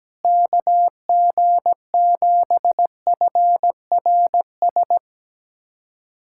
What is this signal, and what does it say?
Morse code, heard as a keyed tone, about 17 wpm, 700 Hz. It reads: KG7FRS
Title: USA Callsign DE Callsign 17wpm